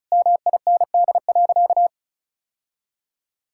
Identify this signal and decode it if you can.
Morse code, heard as a keyed tone, about 36 wpm, 700 Hz. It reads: MIND.
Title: Sentences from Top 1000 Words - Review Part 6 - 35wpm